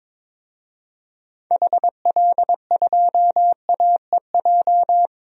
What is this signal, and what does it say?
Morse code, heard as a keyed tone, about 22 wpm, 700 Hz. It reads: HL2AEJ
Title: DXCC-100 Call Signs - 6 Characters - 22wpm